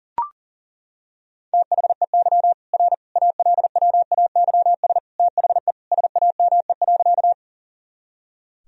Morse code, heard as a keyed tone, about 40 wpm, 700 Hz. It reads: THEY R ALWAYS THE SAME.
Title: Sentences from Top 300 Words and Abbreviations 40wpm